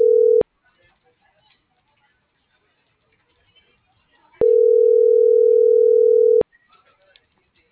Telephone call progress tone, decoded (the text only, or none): ringback tone